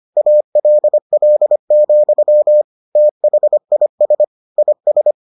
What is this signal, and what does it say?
Morse code, heard as a keyed tone, about 25 wpm, 600 Hz. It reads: ALL, THIS IS